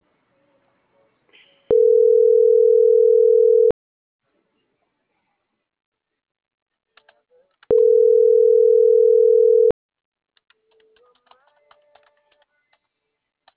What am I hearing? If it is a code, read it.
Ringback tone (North America), 440 + 480 Hz.